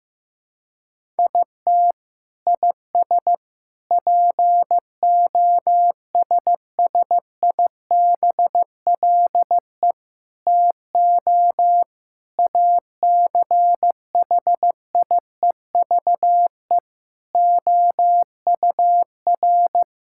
Morse code, heard as a keyed tone, about 15 wpm, 700 Hz. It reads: IT IS POSSIBLE TO ACHIEVE OUR
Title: Sentences from Top 1000 Words - Review Part 6 - 15wpm